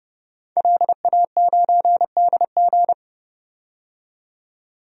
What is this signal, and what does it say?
Morse code, heard as a keyed tone, about 30 wpm, 700 Hz. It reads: LA9DG
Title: DXCC-100 Call Signs - 5 Characters - 30wpm